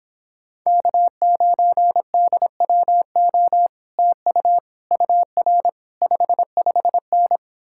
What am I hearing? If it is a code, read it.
Morse code, heard as a keyed tone, about 29 wpm, 700 Hz. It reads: K9DWO TU UR 55N